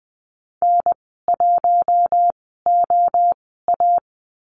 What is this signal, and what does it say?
Morse code, heard as a keyed tone, about 20 wpm, spaced as for 15 wpm, 700 Hz. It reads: N1OA
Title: 048 Lesson 9g - Number 4 - Call Signs 20wpm